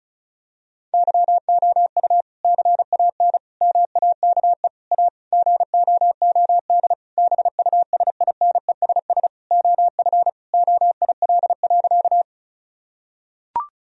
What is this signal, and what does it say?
Morse code, heard as a keyed tone, about 34 wpm, 700 Hz. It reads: YOU CAN MAKE A GOOD BUSINESS OF OIL.
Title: Sentences from Top 1000 Words - Review Part 4 - 35wpm